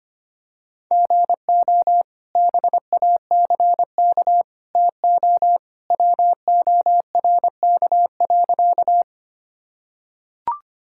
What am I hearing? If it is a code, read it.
Morse code, heard as a keyed tone, about 25 wpm, 700 Hz. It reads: GO BACK TO WORK.
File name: practice / 25wpm / Sentences with 2-Word Phrases - 25wpm